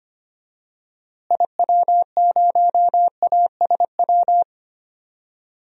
Morse code, heard as a keyed tone, about 25 wpm, 700 Hz. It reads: IW0ASW
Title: DXCC-100 Call Signs - 6 Characters - Repeated 2x - 25wpm